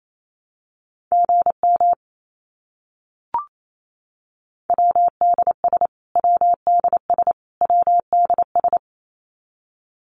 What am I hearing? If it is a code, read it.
Morse code, heard as a keyed tone, about 28 wpm, 700 Hz. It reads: GM WDS WDS WDS
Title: Abbreviations Repeated 3x - 28wpm